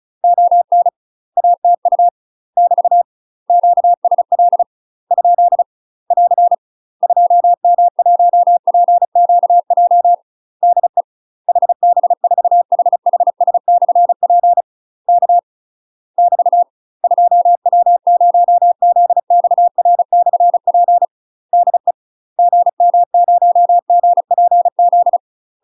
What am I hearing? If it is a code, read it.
Morse code, heard as a keyed tone, about 36 wpm, 700 Hz. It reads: ON ATU <BT> QSL ? <AR> 2M1PQJ DE HB4HHS/P K <BT> 2W0ZXR/P DE GM0GPZ
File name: UK MASTER QSOs TRAINING / 35WPM / UK MASTER QSOs 0033